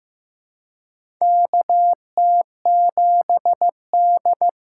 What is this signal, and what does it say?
Morse code, heard as a keyed tone, about 15 wpm, 700 Hz. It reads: KT7D
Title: USA Call Signs - 4x Character Spacing - 15wpm